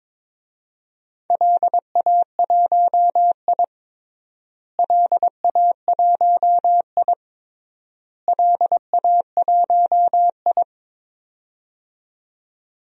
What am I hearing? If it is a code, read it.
Morse code, heard as a keyed tone, about 22 wpm, 700 Hz. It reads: LA1I LA1I LA1I
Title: DXCC-100 Call Signs - 4 Characters - Repeated 3x - 22wpm